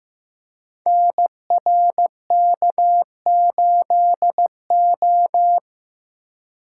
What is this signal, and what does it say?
Morse code, heard as a keyed tone, about 15 wpm, 700 Hz. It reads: NRK8O
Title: Code Groups - 5 Letter:Numbers - 15wpm